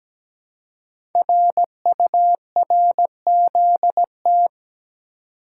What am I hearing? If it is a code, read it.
Morse code, heard as a keyed tone, about 17 wpm, 700 Hz. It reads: RURZT
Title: Code Groups - 5 Letters - 17wpm